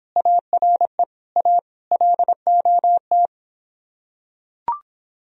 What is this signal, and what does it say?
Morse code, heard as a keyed tone, about 26 wpm, 700 Hz. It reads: ARE A LOT